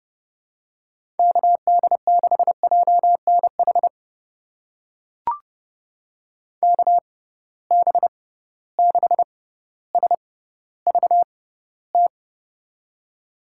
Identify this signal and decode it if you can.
Morse code, heard as a keyed tone, about 31 wpm, spaced as for 16 wpm, 700 Hz. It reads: KD6JNH KB6SVT
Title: USA Call Signs - 6 Characters - 6x Character Spacing - 30wpm